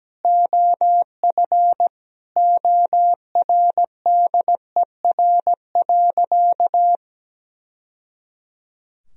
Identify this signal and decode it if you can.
Morse code, heard as a keyed tone, about 17 wpm, 700 Hz. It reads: OF ORDER.
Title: Sentences from Top 1000 Words - Review Part 2 - 17wpm